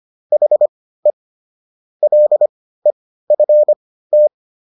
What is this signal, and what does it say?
Morse code, heard as a keyed tone, about 25 wpm, spaced as for 15 wpm, 600 Hz. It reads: HE LEFT